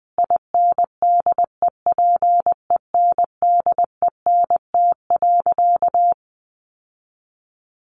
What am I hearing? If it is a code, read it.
Morse code, heard as a keyed tone, about 20 wpm, 700 Hz. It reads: INDEPENDENT.
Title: Sentences from Top 900 Words 20wpm